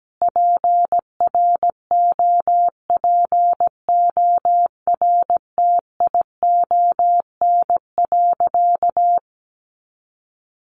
Morse code, heard as a keyed tone, about 17 wpm, 700 Hz. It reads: PROPORTION.
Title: A Sound Of Thunder 17wpm